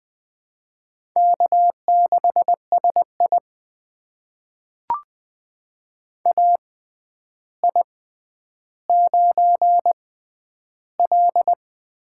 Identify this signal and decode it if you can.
Morse code, heard as a keyed tone, about 20 wpm, spaced as for 10 wpm, 700 Hz. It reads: K6SI AI9L
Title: USA Call Signs - 4 Characters - 6x Character Spacing - 20wpm